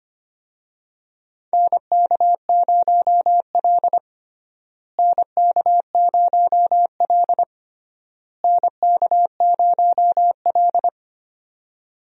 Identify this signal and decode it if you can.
Morse code, heard as a keyed tone, about 25 wpm, 700 Hz. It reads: NK0L NK0L NK0L
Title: USA Call Signs - 4 Characters - Repeated 3x - 25wpm